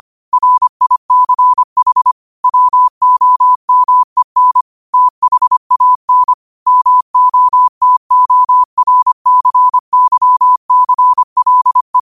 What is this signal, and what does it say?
Morse code, heard as a keyed tone, about 25 wpm, 1000 Hz. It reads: RICH WOMEN THAN MOTORCYCLE
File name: 25wpm